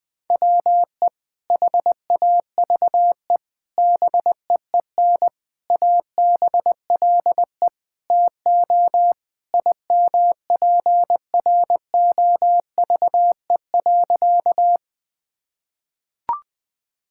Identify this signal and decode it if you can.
Morse code, heard as a keyed tone, about 21 wpm, 700 Hz. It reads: WE HAVE BEEN ABLE TO IMPROVE.
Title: Sentences with 4-Word Phrases - 20wpm